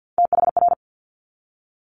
Morse code, heard as a keyed tone, about 50 wpm, 700 Hz. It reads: THR
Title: Multi-letter Phonemes 50wpm